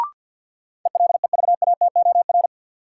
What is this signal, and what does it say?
Morse code, heard as a keyed tone, about 52 wpm, 700 Hz. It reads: ELEVATOR
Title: Sentences from Top 1600 Words - New Word Drill Alt - 50wpm